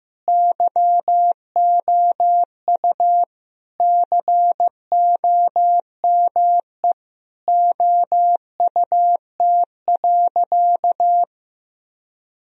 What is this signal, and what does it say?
Morse code, heard as a keyed tone, about 15 wpm, 700 Hz. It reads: YOU COME OUT.
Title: Sentences from Top 1000 Words - Review Part 1 - 15wpm